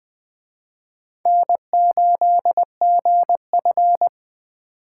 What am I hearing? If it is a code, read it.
Morse code, heard as a keyed tone, about 20 wpm, 700 Hz. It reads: N8GF
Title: Speed Racing - USA 1x2 Call Signs 17wpm